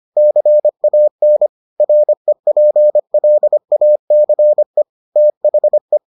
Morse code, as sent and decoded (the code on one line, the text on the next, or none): -.-. .- -. / .-. .--. .-.. .- -.-. / - .... .
CAN REPLACE THE